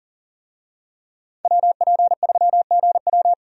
Morse code, heard as a keyed tone, about 40 wpm, 700 Hz. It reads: WP3GW
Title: USA Call Signs - 5 Characters - 40wpm